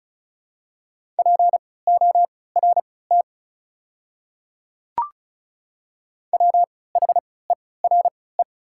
Morse code, heard as a keyed tone, about 36 wpm, spaced as for 20 wpm, 700 Hz. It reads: PORT WHERE
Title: 3x Character Spacing - Top 200 Words - 35wpm